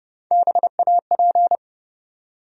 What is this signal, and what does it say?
Morse code, heard as a keyed tone, about 30 wpm, 700 Hz. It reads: BAP
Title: USA Call Sign Postfixes - 3 Letters 30wpm